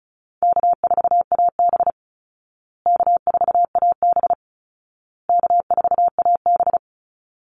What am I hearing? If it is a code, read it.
Morse code, heard as a keyed tone, about 34 wpm, 700 Hz. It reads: K4AB K4AB K4AB
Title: Call Signs Repeated 2x - 35wpm